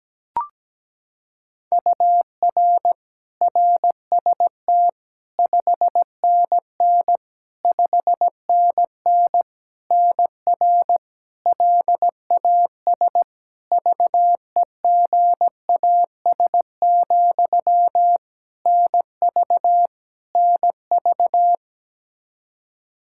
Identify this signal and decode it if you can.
Morse code, heard as a keyed tone, about 17 wpm, 700 Hz. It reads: UR RST 5NN 5NN NR LAS VEGAS, NV NV
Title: K1USN SST QSO Practice 17wpm